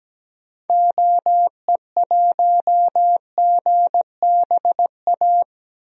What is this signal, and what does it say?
Morse code, heard as a keyed tone, about 17 wpm, 700 Hz. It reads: OE1GBA
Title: DXCC-100 Call Signs - 6 Characters - Repeated 3x - 17wpm